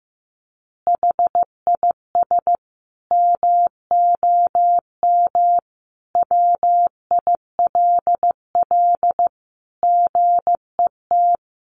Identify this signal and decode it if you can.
Morse code, heard as a keyed tone, about 15 wpm, 700 Hz. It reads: HIS MOM WILL GET
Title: Additional Easy Phrases 15wpm